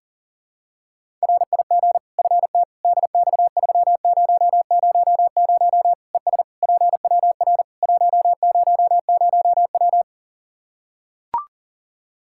Morse code, heard as a keyed tone, about 40 wpm, 700 Hz. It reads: RIG FT DX3000 ES PWR 100W
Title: Practice QSO - Exchange 2 - Info - 40wpm